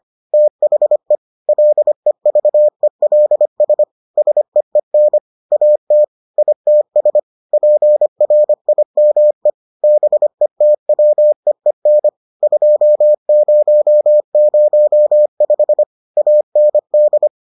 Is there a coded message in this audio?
Yes: Morse code from 0.3 to 17.3 s, heard as a keyed tone, about 25 wpm, 600 Hz. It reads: THE LEVELS SEEN AT ITS PRIME BETWEEN 2005 AND